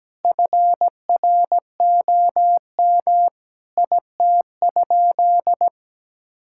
Morse code, heard as a keyed tone, about 17 wpm, 700 Hz. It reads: FROM IT?
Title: Sentences from Top 1000 Words - Review Part 5 - 17wpm